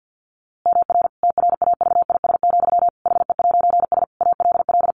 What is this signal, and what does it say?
Morse code, heard as a keyed tone, about 50 wpm, 700 Hz. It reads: MR TRAVIS, HE'S ALL
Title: A Sound Of Thunder 50wpm